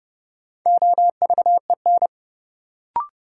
Morse code, heard as a keyed tone, about 32 wpm, 700 Hz. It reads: OVEN
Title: Sentences from Top 1200 Words - New Word Drill Alt - 30wpm